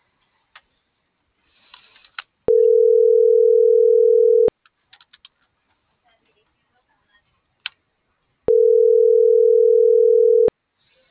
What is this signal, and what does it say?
Ringback tone (North America), 440 + 480 Hz.